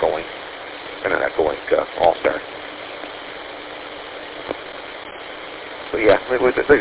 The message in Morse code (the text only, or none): none